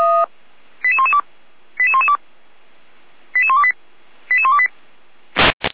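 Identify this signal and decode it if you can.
Five-tone selective call (ZVEI), heard as quick runs of five tones, tones of about 70 ms. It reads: 88112 88112 88128 88128
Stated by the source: Five-tone ZVEI selective call received by SDR.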